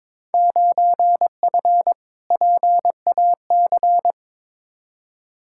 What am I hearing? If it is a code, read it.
Morse code, heard as a keyed tone, about 21 wpm, 700 Hz. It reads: 9F PAC
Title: Field Day Exchanges Encore - 22wpm